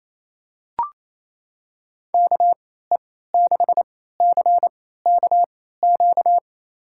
Morse code, heard as a keyed tone, about 28 wpm, spaced as for 16 wpm, 700 Hz. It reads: KE6CKQ
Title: USA Call Signs - 6 Characters - 3x Character Spacing - 28wpm